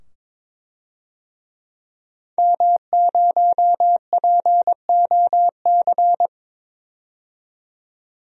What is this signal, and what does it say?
Morse code, heard as a keyed tone, about 22 wpm, 700 Hz. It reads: M0POC
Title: DXCC-100 Call Signs - 5 Characters - 22wpm